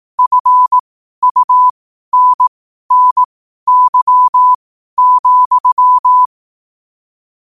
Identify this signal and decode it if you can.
Morse code, heard as a keyed tone, about 18 wpm, spaced as for 12 wpm, 1000 Hz. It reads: FUNNY,
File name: qotdfiles / 12wpm